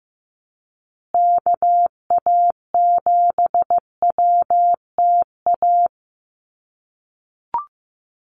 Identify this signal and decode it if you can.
Morse code, heard as a keyed tone, about 15 wpm, 700 Hz. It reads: KA7WTA
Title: USA Call Signs Repeated 3x 15wpm